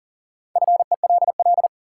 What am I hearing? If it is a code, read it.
Morse code, heard as a keyed tone, about 37 wpm, 700 Hz. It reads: FELL
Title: Sets of 5 Words Encore - New Word Drill - 40wpm